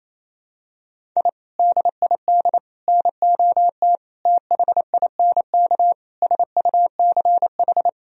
Morse code, heard as a keyed tone, about 28 wpm, 700 Hz. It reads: I DID NOT THINK SUCH
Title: Sentences from Top 1000 Words - Review Part 1 - 28wpm